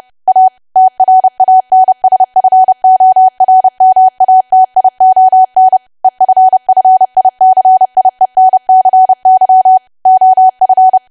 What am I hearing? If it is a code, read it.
Morse code, heard as a keyed tone, about 30 wpm, 750 Hz. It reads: A TRANSFORMATION EFFICIENCY OF
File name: cwa / W1AW-Level3 / 30WPM